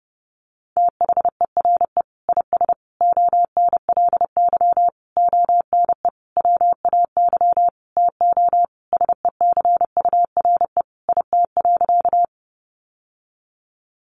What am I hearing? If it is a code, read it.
Morse code, heard as a keyed tone, about 30 wpm, 700 Hz. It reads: THERE IS ONLY ONE WAY TO SECURE IT.
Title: Sentences from Top 1200 Words 30wpm